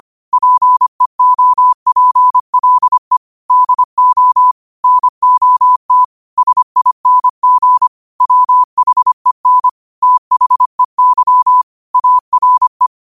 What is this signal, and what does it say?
Morse code, heard as a keyed tone, about 25 wpm, 1000 Hz. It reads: PEOPLE DO NOT SING WHEN THEY ARE
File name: qotdfiles / 25wpm